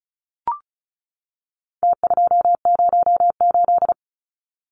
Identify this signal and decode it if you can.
Morse code, heard as a keyed tone, about 34 wpm, 700 Hz. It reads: T208
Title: Numbers - Contest Style - 4-Digit 35wpm